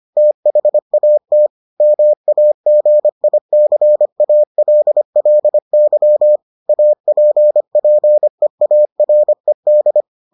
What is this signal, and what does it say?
Morse code, heard as a keyed tone, about 25 wpm, 600 Hz. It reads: THAT MAGICALLY APPEARED